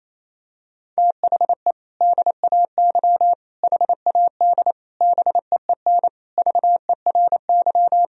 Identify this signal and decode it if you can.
Morse code, heard as a keyed tone, about 28 wpm, 700 Hz. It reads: THE DAY HAD BEEN VERY
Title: Sentences from Top 1000 Words - Review Part 2 - 28wpm